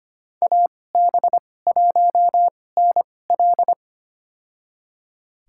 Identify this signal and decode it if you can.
Morse code, heard as a keyed tone, about 25 wpm, spaced as for 18 wpm, 700 Hz. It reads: AB1NL
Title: USA Call Signs - 2x Character Spacing - 25wpm